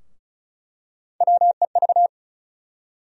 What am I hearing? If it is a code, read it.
Morse code, heard as a keyed tone, about 34 wpm, 700 Hz. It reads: WEV
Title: USA Call Sign Postfixes - 3 Letters 35wpm